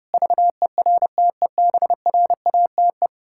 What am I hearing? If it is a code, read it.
Morse code, heard as a keyed tone, about 28 wpm, 700 Hz. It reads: VERTEBRATE
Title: Pangram Challenge v2 - 30wpm